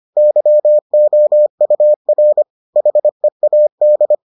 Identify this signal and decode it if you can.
Morse code, heard as a keyed tone, about 25 wpm, 600 Hz. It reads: YOUR HEAD